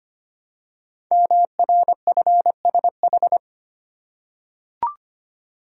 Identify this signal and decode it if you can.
Morse code, heard as a keyed tone, about 25 wpm, 700 Hz. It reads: MRFSH
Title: Code Groups - 5 Letters - 25wpm